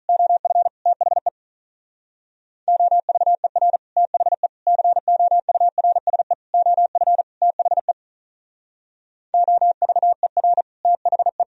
Morse code, heard as a keyed tone, about 39 wpm, 700 Hz. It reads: OF THE OVER THE COURSE OF THE OVER THE
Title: Speed-Racing - Sets of 5 Words 35wpm